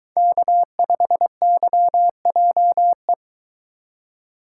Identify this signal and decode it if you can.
Morse code, heard as a keyed tone, about 23 wpm, 700 Hz. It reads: K5YJE